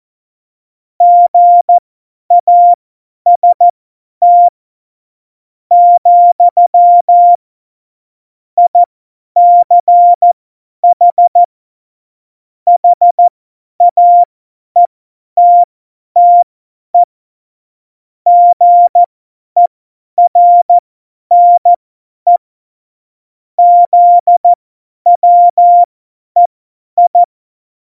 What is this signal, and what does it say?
Morse code, heard as a keyed tone, about 14 wpm, spaced as for 10 wpm, 700 Hz. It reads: GAST, ICH HAETTE GERNE ZWEI
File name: Morsekurs / Witze / Witz 02. April 2019, 50 F70